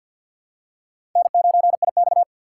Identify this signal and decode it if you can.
Morse code, heard as a keyed tone, about 53 wpm, 700 Hz. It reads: N9IX
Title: USA Call Signs - 4 Characters - 6x Character Spacing - 50wpm